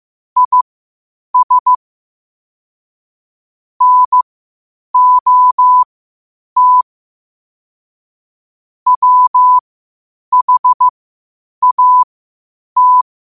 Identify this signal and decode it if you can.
Morse code, heard as a keyed tone, about 15 wpm, spaced as for 8 wpm, 1000 Hz. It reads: IS NOT WHAT
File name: 8wpm